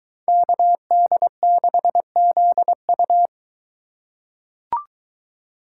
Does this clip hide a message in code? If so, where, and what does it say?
Yes: Morse code from 0.3 to 3.3 s, heard as a keyed tone, about 21 wpm, 700 Hz. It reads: KD6ZU